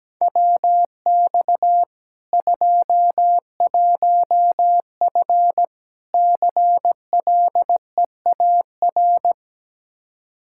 Morse code, heard as a keyed tone, about 17 wpm, 700 Hz. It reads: WX 21F CLEAR
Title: K1USN SST QSO Practice 17wpm